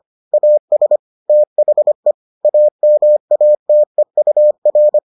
Morse code, heard as a keyed tone, about 25 wpm, 600 Hz. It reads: AS THE AMATEUR